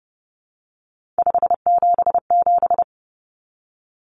Morse code, heard as a keyed tone, about 30 wpm, 700 Hz. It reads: 577
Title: Common Cut Numbers - 3-Digit 30wpm